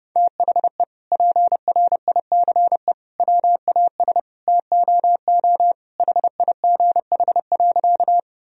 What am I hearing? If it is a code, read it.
Morse code, heard as a keyed tone, about 30 wpm, 700 Hz. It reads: THE PRICE WAS TOO HIGH.
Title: Sentences from Top 1000 Words - Review Part 4 - 30wpm